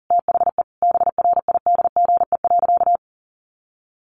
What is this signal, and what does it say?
Morse code, heard as a keyed tone, about 40 wpm, 700 Hz. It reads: THE BRIDGE.
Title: Sentences from Top 1200 Words 40wpm